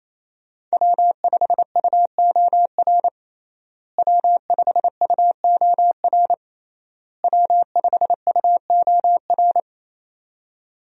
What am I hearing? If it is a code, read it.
Morse code, heard as a keyed tone, about 28 wpm, 700 Hz. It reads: W5UOR W5UOR W5UOR
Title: USA Call Signs - 5 Characters - Repeated 3x - 28wpm